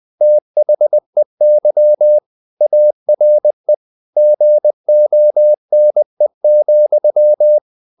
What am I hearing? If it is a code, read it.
Morse code, heard as a keyed tone, about 20 wpm, 600 Hz. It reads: THEY ARE GONE,
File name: CW / Practice / If20